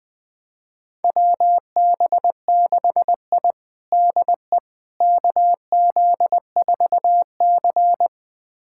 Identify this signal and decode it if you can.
Morse code, heard as a keyed tone, about 21 wpm, 700 Hz. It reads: WB6I DE KZ4C
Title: USA Callsign DE Callsign - 4 Characters - 20wpm